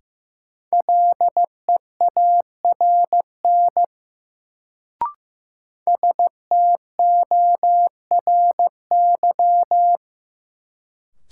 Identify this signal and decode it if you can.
Morse code, heard as a keyed tone, about 15 wpm, 700 Hz. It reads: LEARN STORY
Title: Sentences from Top 300 Words - New Word Drill - 15wpm